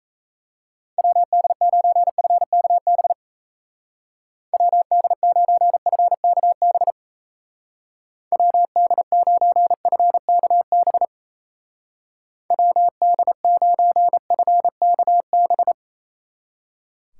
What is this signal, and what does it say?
Morse code, heard as a keyed tone, about 35 wpm, 700 Hz. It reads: WD9FKB WD9FKB WD9FKB WD9FKB
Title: Speed Racing - USA 2x3 Call Signs 28wpm